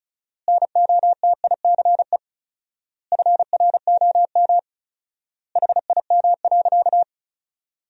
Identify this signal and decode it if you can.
Morse code, heard as a keyed tone, about 36 wpm, 700 Hz. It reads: NOTICE FROM HIM.
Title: 4x Word-Spacing - Sentences from Top 400 Words - 35wpm